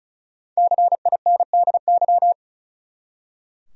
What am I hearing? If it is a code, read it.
Morse code, heard as a keyed tone, about 34 wpm, 700 Hz. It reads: CINDY
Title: USA Common Names - Max 5 Letters - 35wpm